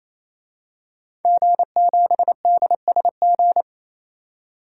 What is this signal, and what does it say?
Morse code, heard as a keyed tone, about 28 wpm, 700 Hz. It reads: G7DSG
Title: DXCC-100 Call Signs - 5 Characters - 28wpm